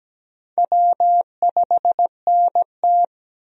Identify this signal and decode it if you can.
Morse code, heard as a keyed tone, about 17 wpm, 700 Hz. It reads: W5NT
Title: USA Call Signs - 4 Characters - 17wpm